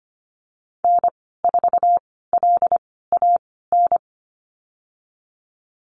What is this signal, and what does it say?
Morse code, heard as a keyed tone, about 25 wpm, spaced as for 16 wpm, 700 Hz. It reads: N4LAN
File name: Lesson 15g - Number 5 - Call Signs 25wpm